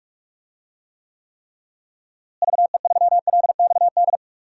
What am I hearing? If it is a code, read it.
Morse code, heard as a keyed tone, about 45 wpm, 700 Hz. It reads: VE3LXD
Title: DXCC-100 Call Signs - 6 Characters - Repeated 3x - 45wpm